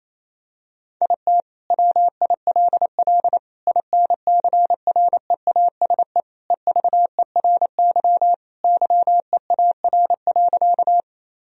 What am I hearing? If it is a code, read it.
Morse code, heard as a keyed tone, about 28 wpm, 700 Hz. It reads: IT WILL INCREASE EVERY YEAR.
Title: Sentences from Top 1000 Words - Review Part 4 - 28wpm